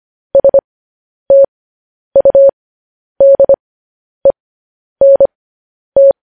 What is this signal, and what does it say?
Morse code, heard as a keyed tone, about 25 wpm, spaced as for 10 wpm, 550 Hz. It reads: STUDENT